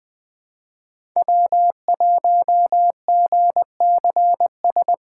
Morse code, heard as a keyed tone, about 20 wpm, 700 Hz. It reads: W1GCS
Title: K1USN SST QSO Practice - 20wpm